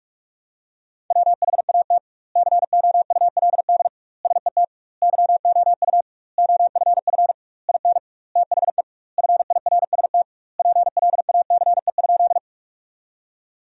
Morse code, heard as a keyed tone, about 46 wpm, 700 Hz. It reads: WHAT COULD SET YOU OFF IN THE FIRST PLACE?
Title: 2x Word-Spacing - Sentences from Top 200 Words - 45wpm